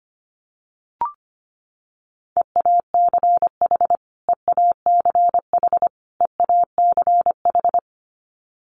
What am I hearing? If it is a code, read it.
Morse code, heard as a keyed tone, about 25 wpm, 700 Hz. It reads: EACH EACH EACH
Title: Top 300 Words Repeated 25wpm